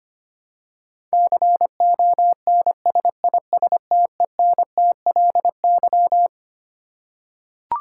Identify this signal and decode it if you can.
Morse code, heard as a keyed tone, about 25 wpm, 700 Hz. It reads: CONSISTENTLY
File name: Twelve Letter Words - 25wpm